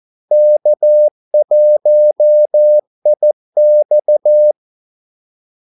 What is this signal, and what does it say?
Morse code, heard as a keyed tone, about 14 wpm, 600 Hz. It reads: K1IX
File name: Callsigns-14wpm